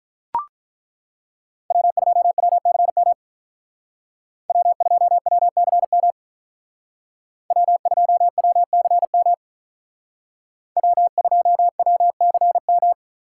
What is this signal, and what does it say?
Morse code, heard as a keyed tone, about 43 wpm, 700 Hz. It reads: W2WCM W2WCM W2WCM W2WCM
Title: Speed Racing - USA 1x3 Call Signs 35wpm